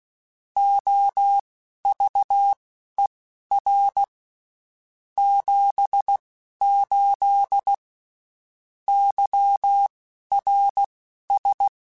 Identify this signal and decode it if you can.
Morse code, heard as a keyed tone, about 16 wpm, spaced as for 11 wpm, 800 Hz. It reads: OVER 78 YRS
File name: (spaced) / QSOs Spaced-mp3s / kc7haa @15 sp